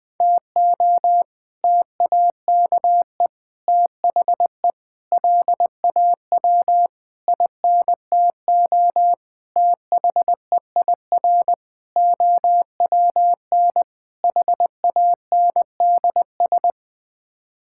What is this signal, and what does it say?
Morse code, heard as a keyed tone, about 20 wpm, 700 Hz. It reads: TO TAKE THE LAW INTO THEIR OWN HANDS
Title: Sets of 8 Words - 20wpm